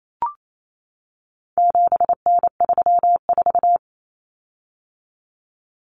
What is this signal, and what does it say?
Morse code, heard as a keyed tone, about 28 wpm, 700 Hz. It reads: 7N34
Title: Common Cut Numbers - 4-Digit 28wpm